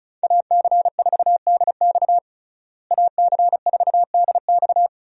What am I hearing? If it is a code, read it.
Morse code, heard as a keyed tone, about 34 wpm, 700 Hz. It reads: AC4DX AC4DX
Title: USA Call Signs - 5 Characters - Repeated 3x - 35wpm